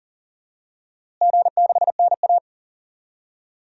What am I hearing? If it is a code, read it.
Morse code, heard as a keyed tone, about 40 wpm, 700 Hz. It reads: G6NA
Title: DXCC-100 Call Signs - 4 Characters - Repeated 2x - 40wpm